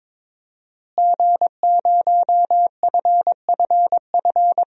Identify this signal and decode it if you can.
Morse code, heard as a keyed tone, about 22 wpm, 700 Hz. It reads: G0FFF
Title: DXCC-100 Call Signs - 5 Characters - Repeated 3x - 22wpm